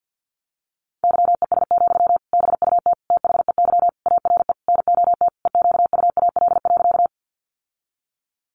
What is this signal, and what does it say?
Morse code, heard as a keyed tone, about 50 wpm, 700 Hz. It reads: YES, BUT THEY ARE NOT EQUAL.
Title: Sentences from Top 1400 Words 50wpm